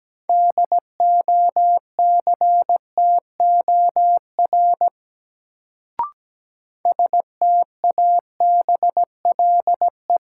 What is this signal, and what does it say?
Morse code, heard as a keyed tone, about 17 wpm, 700 Hz. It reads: DOCTOR STABLE
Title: Sentences from Top 1500 Words - New Word Drill Alt - 17wpm